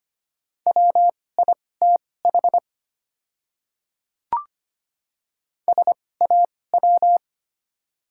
Morse code, heard as a keyed tone, about 25 wpm, spaced as for 18 wpm, 700 Hz. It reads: WITH SAW
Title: 2x Character Spacing - Top 500 Words - 25wpm